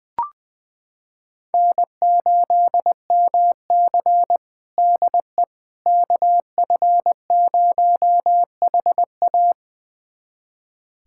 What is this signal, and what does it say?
Morse code, heard as a keyed tone, about 20 wpm, 700 Hz. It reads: N8MC DE KF0HA
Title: USA Callsign DE Callsign 20wpm